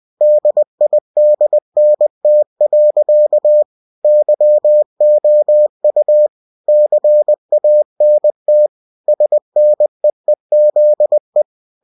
Morse code, heard as a keyed tone, about 20 wpm, 600 Hz. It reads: DIDNT. YOU CANT SNEEZE
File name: cw-books / winnie-the-pooh / winnie-20wpm-page-0030